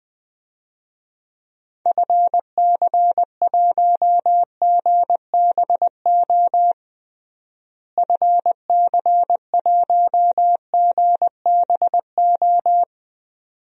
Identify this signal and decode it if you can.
Morse code, heard as a keyed tone, about 20 wpm, 700 Hz. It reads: FC1GBO FC1GBO
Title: DXCC-100 Call Signs - 6 Characters - Repeated 2x - 20wpm